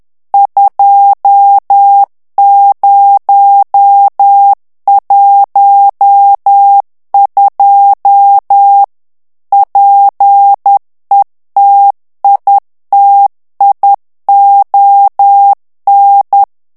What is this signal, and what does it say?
Morse code, heard as a keyed tone, about 10 wpm, 800 Hz. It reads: 2012 PETITION
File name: ANGLAIS texte clair 3 vitesse 10 mots minute